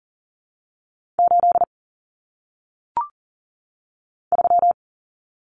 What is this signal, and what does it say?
Morse code, heard as a keyed tone, about 40 wpm, 700 Hz. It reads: 8 3
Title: Numbers - 1-Digit 40wpm